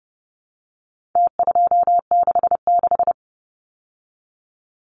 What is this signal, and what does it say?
Morse code, heard as a keyed tone, about 30 wpm, 700 Hz. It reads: T266
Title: Numbers - Contest Style - 4-Digit 30wpm